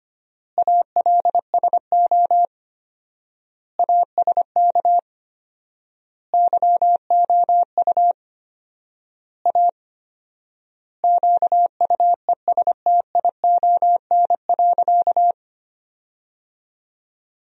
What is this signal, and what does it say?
Morse code, heard as a keyed tone, about 25 wpm, 700 Hz. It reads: ALSO ASK YOU A QUESTION.
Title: 4x Word-Spacing - Sentences from Top 400 Words - 25wpm